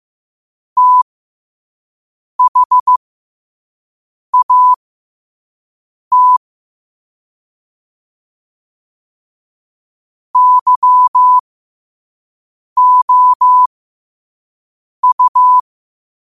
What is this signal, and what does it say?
Morse code, heard as a keyed tone, about 15 wpm, spaced as for 5 wpm, 1000 Hz. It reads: THAT YOU
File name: qotdfiles / 5wpm